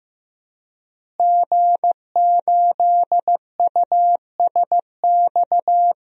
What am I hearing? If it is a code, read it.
Morse code, heard as a keyed tone, about 15 wpm, 700 Hz. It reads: G8USX
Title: DXCC-100 Call Signs - 5 Characters - Repeated 3x - 15wpm